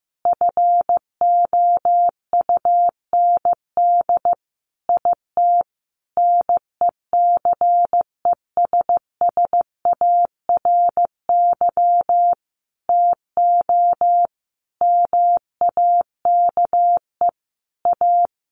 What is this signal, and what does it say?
Morse code, heard as a keyed tone, about 15 wpm, 700 Hz. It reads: FOUND IT NECESSARY TO MAKE A
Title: Sentences from Top 1400 Words 15wpm